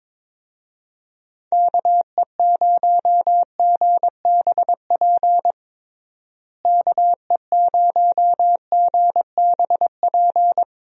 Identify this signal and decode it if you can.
Morse code, heard as a keyed tone, about 22 wpm, 700 Hz. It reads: KE0GBP KE0GBP
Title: USA Call Signs - 6 Characters - Repeated 3x - 22wpm